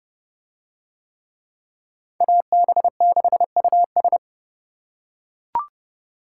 Morse code, heard as a keyed tone, about 27 wpm, 700 Hz. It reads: AB6US